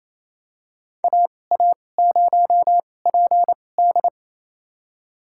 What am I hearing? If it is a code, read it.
Morse code, heard as a keyed tone, about 28 wpm, spaced as for 20 wpm, 700 Hz. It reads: AA0PD
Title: USA Call Signs - 5 Characters - 2x Character Spacing - 28wpm